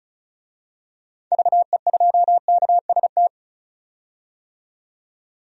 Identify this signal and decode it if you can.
Morse code, heard as a keyed tone, about 34 wpm, 700 Hz. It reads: VE2KST
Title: DXCC-100 Call Signs - 6 Characters - 35wpm